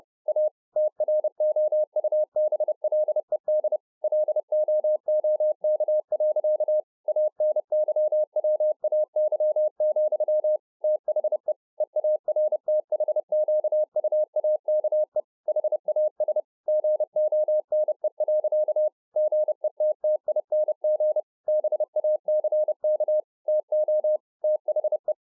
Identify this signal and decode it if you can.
Morse code, heard as a keyed tone, about 30 wpm, 610 Hz. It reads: A TROUBLED LOOK. ANYWAY, THE EARTHQUAKE HAS GONE. GETTING BACK TO THE